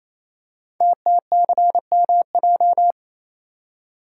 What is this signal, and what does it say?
Morse code, heard as a keyed tone, about 27 wpm, 700 Hz. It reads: TTCMJ
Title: Code Groups - 5 Letters - 28wpm